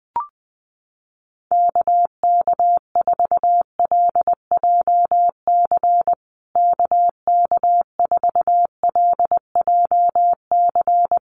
Morse code, heard as a keyed tone, about 20 wpm, 700 Hz. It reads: KK4LJC KK4LJC
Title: USA Call Signs Repeated 3x 20wpm